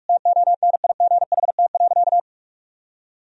Morse code, heard as a keyed tone, about 41 wpm, 700 Hz. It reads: TONIGHT.
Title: Sentences from Top 1000 Words - Review Part 3 - 45wpm